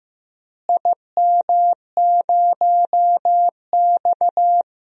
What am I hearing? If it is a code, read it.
Morse code, heard as a keyed tone, about 15 wpm, 700 Hz. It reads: IM0X
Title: DXCC-100 Call Signs - 4 Characters - Repeated 3x - 15wpm